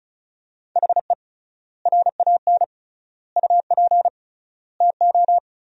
Morse code, heard as a keyed tone, about 36 wpm, 700 Hz. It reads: HE RAN UP TO
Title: 3x Word-Spacing - Sentences from Top 500 Words - 35wpm